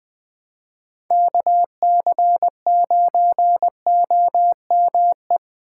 Morse code, heard as a keyed tone, about 21 wpm, 700 Hz. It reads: KC9OME
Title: USA Call Signs - 2x Character Spacing - 20wpm